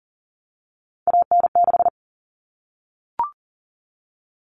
Morse code, heard as a keyed tone, about 40 wpm, 700 Hz. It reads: AN6
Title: All Cut Numbers - 3-Digit 40wpm